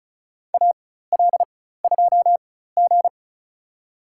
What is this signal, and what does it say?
Morse code, heard as a keyed tone, about 34 wpm, spaced as for 16 wpm, 700 Hz. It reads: AL2G
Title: USA Call Signs - 4 Characters - 4x Character Spacing - 35wpm